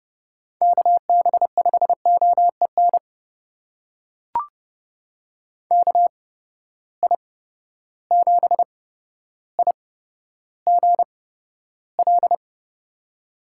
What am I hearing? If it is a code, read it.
Morse code, heard as a keyed tone, about 31 wpm, spaced as for 13 wpm, 700 Hz. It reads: KB5OEN KI7IGL
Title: USA Call Signs - 6 Characters - 8x Character Spacing - 30wpm